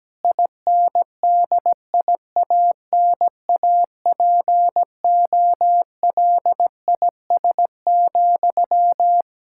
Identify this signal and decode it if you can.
Morse code, heard as a keyed tone, about 17 wpm, 700 Hz. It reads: INDIANAPOLIS,
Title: K1USN SST QSO Practice 17wpm